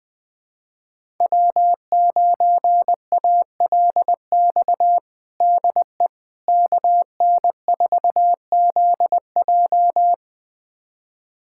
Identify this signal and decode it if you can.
Morse code, heard as a keyed tone, about 21 wpm, 700 Hz. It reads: W9ALX DE KN4ZJ
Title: USA Callsign DE Callsign - 5 Characters - 20wpm